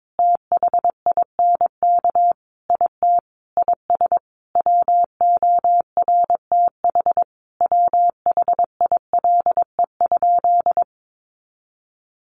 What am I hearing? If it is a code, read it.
Morse code, heard as a keyed tone, about 22 wpm, 700 Hz. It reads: THINK IT IS WORTH WHILE?
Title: Sentences from Top 800 Words 22wpm